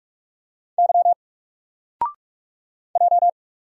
Morse code, heard as a keyed tone, about 45 wpm, 700 Hz. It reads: Y J
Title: Single Letter-Number Alt 45wpm